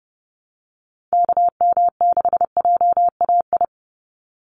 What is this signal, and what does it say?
Morse code, heard as a keyed tone, about 30 wpm, 700 Hz. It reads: KM6JAI
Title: USA Call Signs 30wpm